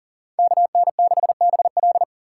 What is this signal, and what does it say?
Morse code, heard as a keyed tone, about 50 wpm, 700 Hz. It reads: KN6BL